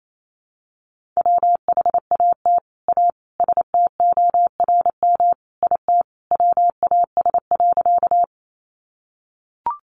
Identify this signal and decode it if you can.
Morse code, heard as a keyed tone, about 28 wpm, 700 Hz. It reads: WHAT A STORM IT WAS.
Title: Sentences from Top 1500 Words 28wpm